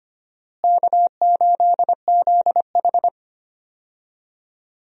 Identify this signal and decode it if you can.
Morse code, heard as a keyed tone, about 25 wpm, 700 Hz. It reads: K8ZH
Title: USA Call Signs - 4 Characters - Repeated 3x - 25wpm